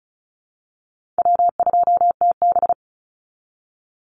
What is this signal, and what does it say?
Morse code, heard as a keyed tone, about 34 wpm, 700 Hz. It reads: W2TB
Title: Call Signs Repeated 3x - 35wpm